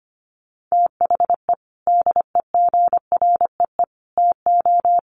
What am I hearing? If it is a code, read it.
Morse code, heard as a keyed tone, about 25 wpm, 700 Hz. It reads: THE DEGREE TO
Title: Sets of 5 Words Encore2 25wpm